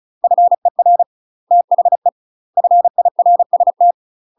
Morse code, heard as a keyed tone, about 34 wpm, 700 Hz. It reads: FER THE FIRST
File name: UK MASTER QSOs TRAINING / 35WPM / UK MASTER QSOs 0015